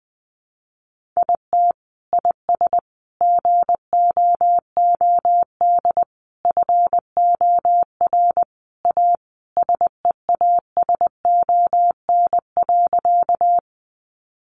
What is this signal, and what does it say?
Morse code, heard as a keyed tone, about 20 wpm, 700 Hz. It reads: IT IS GOOD FOR A SEASON.
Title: Sentences from Top 1400 Words 20wpm